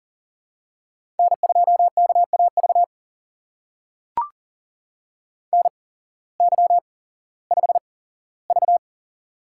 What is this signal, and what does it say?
Morse code, heard as a keyed tone, about 40 wpm, spaced as for 19 wpm, 700 Hz. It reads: N2KAV NY5V
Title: USA Call Signs - 8x Character Spacing - 40wpm